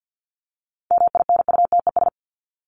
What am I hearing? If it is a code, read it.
Morse code, heard as a keyed tone, about 48 wpm, 700 Hz. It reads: MINUTES
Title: ES:S Suffix - Top 100 Words with Base Words 50wpm